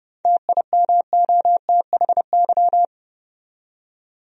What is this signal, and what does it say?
Morse code, heard as a keyed tone, about 30 wpm, 700 Hz. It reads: TIMOTHY
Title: USA Common Names - 30wpm